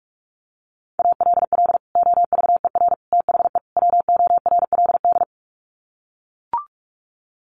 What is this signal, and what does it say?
Morse code, heard as a keyed tone, about 44 wpm, 700 Hz. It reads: ALL OVER THE WORLD
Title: Sets of 4 Words 45wpm